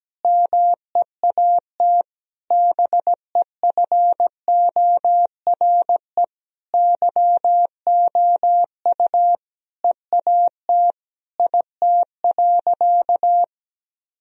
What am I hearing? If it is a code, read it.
Morse code, heard as a keyed tone, about 17 wpm, 700 Hz. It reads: MEAT BEFORE YOU EAT IT.
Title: Sentences from Top 1000 Words - Review Part 3 - 17wpm